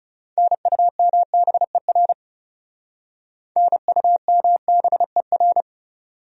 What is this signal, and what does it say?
Morse code, heard as a keyed tone, about 38 wpm, 700 Hz. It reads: NUMBER NUMBER